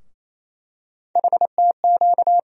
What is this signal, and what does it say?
Morse code, heard as a keyed tone, about 27 wpm, 700 Hz. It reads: HTQ
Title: USA Call Sign Postfixes - 3 Letters 28wpm